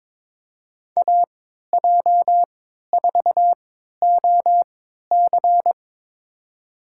Morse code, heard as a keyed tone, about 22 wpm, spaced as for 13 wpm, 700 Hz. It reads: AJ4OC
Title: USA Call Signs - 5 Characters - 3x Character Spacing - 22wpm